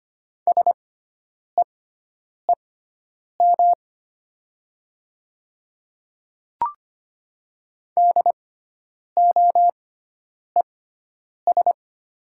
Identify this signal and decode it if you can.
Morse code, heard as a keyed tone, about 25 wpm, spaced as for 9 wpm, 700 Hz. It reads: SEEM DOES
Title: 6x Character Spacing - Top 300 Words - 25wpm